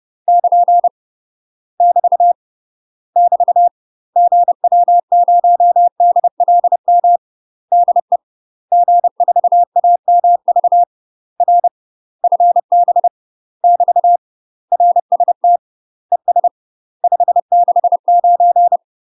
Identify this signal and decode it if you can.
Morse code, heard as a keyed tone, about 30 wpm, 700 Hz. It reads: <KN> <BT> <BT> GW0DLM DE G4AMV R FB <BT> RST ES 569
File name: UK MASTER QSOs TRAINING / 30WPM / UK MASTER QSOs 0016